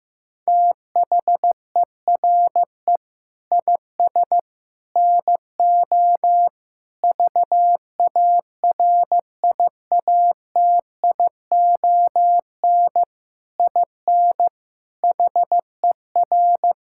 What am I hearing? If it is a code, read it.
Morse code, heard as a keyed tone, about 15 wpm, 700 Hz. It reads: THERE IS NO VARIATION IN HER
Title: Sentences from Top 1600 Words 15wpm